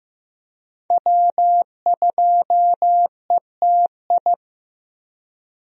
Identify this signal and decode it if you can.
Morse code, heard as a keyed tone, about 15 wpm, 700 Hz. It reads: W2ETI
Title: USA Call Signs - 5 Characters - Repeated 2x - 15wpm